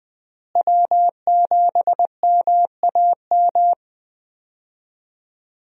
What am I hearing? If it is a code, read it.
Morse code, heard as a keyed tone, about 21 wpm, 700 Hz. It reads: W7MAM
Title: USA Call Signs - 5 Characters - Repeated 3x - 20wpm